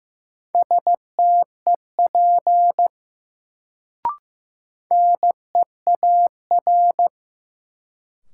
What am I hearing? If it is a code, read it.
Morse code, heard as a keyed tone, about 15 wpm, 700 Hz. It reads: STEP NEAR
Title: Sets of 5 Words Encore - New Word Drill - 15wpm